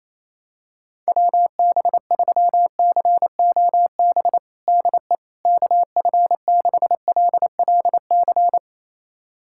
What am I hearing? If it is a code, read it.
Morse code, heard as a keyed tone, about 28 wpm, 700 Hz. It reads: WB3COB DE KF6LLC
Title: USA Callsign DE Callsign - 6 Characters - 28wpm